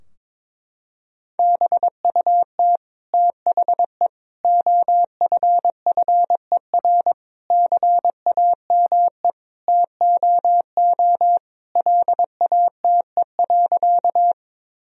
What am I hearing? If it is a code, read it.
Morse code, heard as a keyed tone, about 22 wpm, 700 Hz. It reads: BUT THE OFFER CAME TOO LATE.
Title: Sentences from Top 1000 Words - Review Part 4 - 22wpm